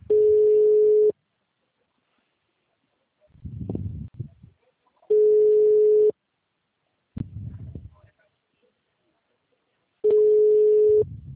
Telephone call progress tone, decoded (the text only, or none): ringback tone